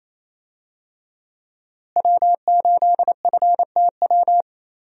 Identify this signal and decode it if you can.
Morse code, heard as a keyed tone, about 28 wpm, 700 Hz. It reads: W8FTW
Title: USA Call Signs - 5 Characters - Repeated 2x - 28wpm